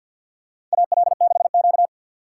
Morse code, heard as a keyed tone, about 53 wpm, 700 Hz. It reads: AL6Q
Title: USA Call Signs - 4 Characters - 6x Character Spacing - 50wpm